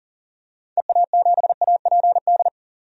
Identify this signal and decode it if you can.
Morse code, heard as a keyed tone, about 40 wpm, 700 Hz. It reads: EA7APD
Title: DXCC-100 Call Signs - 6 Characters - 40wpm